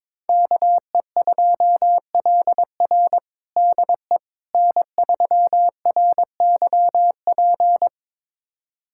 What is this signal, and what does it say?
Morse code, heard as a keyed tone, about 22 wpm, 700 Hz. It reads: KE2LR DE N3RYP
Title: USA Callsign DE Callsign - 5 Characters - 22wpm